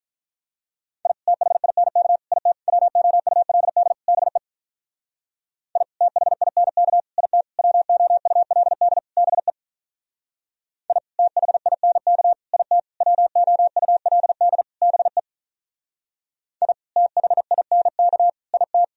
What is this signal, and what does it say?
Morse code, heard as a keyed tone, about 42 wpm, 700 Hz. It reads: I THINK IT WOULD BE I THINK IT WOULD BE I THINK IT WOULD BE I THINK IT
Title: Speed-Racing - Sets of 5 Words 35wpm